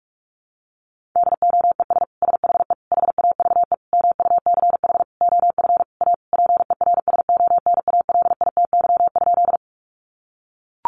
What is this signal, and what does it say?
Morse code, heard as a keyed tone, about 44 wpm, 700 Hz. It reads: DOES SHE HAVE MUCH OF A PERSONALITY?
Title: Sentences from Top 1400 Words 45wpm